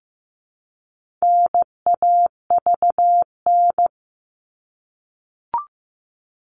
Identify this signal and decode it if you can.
Morse code, heard as a keyed tone, about 15 wpm, 700 Hz. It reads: NAVN
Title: All Cut Numbers - 4-Digit 15wpm